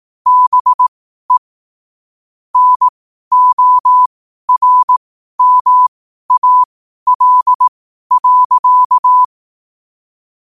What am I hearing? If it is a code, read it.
Morse code, heard as a keyed tone, about 18 wpm, spaced as for 12 wpm, 1000 Hz. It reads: BE NORMAL.
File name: qotdfiles / 12wpm